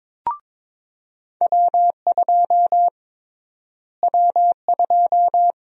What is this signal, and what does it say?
Morse code, heard as a keyed tone, about 22 wpm, 700 Hz. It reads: W2 W2
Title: USA Call Sign Prefixes - Repeated 2x - 22wpm